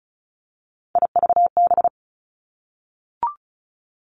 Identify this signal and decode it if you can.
Morse code, heard as a keyed tone, about 36 wpm, 700 Hz. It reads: IVB
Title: Three Challenging Letter-Number Combinations 35wpm